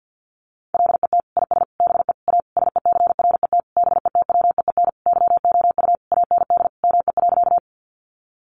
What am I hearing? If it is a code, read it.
Morse code, heard as a keyed tone, about 50 wpm, 700 Hz. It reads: LET IS BE A SECRET BETWEEN YOU AND ME.
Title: Sentences from Top 1200 Words 50wpm